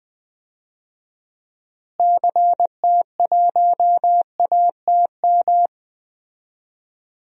Morse code, heard as a keyed tone, about 20 wpm, 700 Hz. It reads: CT1ATM
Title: DXCC-100 Call Signs - 6 Characters - Repeated 2x - 20wpm